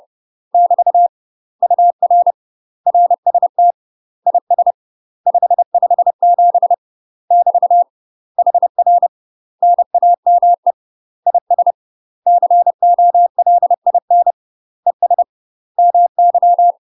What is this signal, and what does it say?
Morse code, heard as a keyed tone, about 30 wpm, 700 Hz. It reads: <BT> UR RST IS 557 <BT> HR NAME IS COLIN ES MY
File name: UK MASTER QSOs TRAINING / 30WPM / UK MASTER QSOs 0049